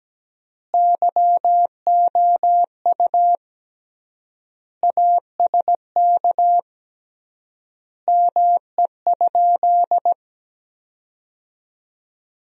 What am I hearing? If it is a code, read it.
Morse code, heard as a keyed tone, about 17 wpm, 700 Hz. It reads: YOU ASK ME?
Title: 3x Word-Spacing - Sentences from Top 200 Words - 17wpm